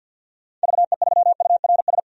Morse code, heard as a keyed tone, about 50 wpm, 700 Hz. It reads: VE3URS
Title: DXCC-100 Call Signs - 6 Characters - 50wpm